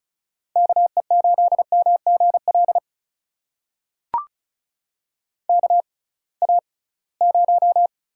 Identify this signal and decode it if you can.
Morse code, heard as a keyed tone, about 36 wpm, spaced as for 23 wpm, 700 Hz. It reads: KE8MGL KA0
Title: USA Call Signs - 6 Characters - 6x Character Spacing - 35wpm